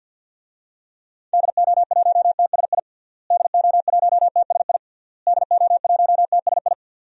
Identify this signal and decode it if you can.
Morse code, heard as a keyed tone, about 53 wpm, 700 Hz. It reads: DO1TSI DO1TSI DO1TSI
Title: DXCC-100 Call Signs - 6 Characters - Repeated 3x - 50wpm